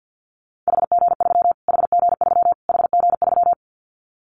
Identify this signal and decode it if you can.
Morse code, heard as a keyed tone, about 50 wpm, 700 Hz. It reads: HG3 HG3 HG3
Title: Callsign Prefixes for Top 25 DXCC - Repeated 3x - 50wpm